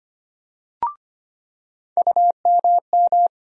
Morse code, heard as a keyed tone, about 25 wpm, 700 Hz. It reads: UMM
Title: USA Call Sign Postfixes - 3 Letters 25wpm